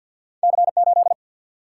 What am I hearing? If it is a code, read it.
Morse code, heard as a keyed tone, about 48 wpm, 700 Hz. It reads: C 8
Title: Single-Letter Single-Number 50wpm